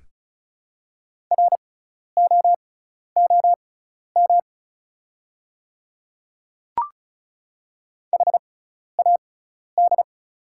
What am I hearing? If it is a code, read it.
Morse code, heard as a keyed tone, about 36 wpm, spaced as for 12 wpm, 690 Hz. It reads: ROOM HAD
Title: 6x Character Spacing - Top 300 Words - 35wpm